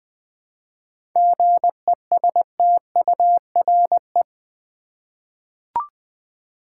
Morse code, heard as a keyed tone, about 20 wpm, 690 Hz. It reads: GESTURE
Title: 7 Letter Words - First 3 Letters Given - 20wpm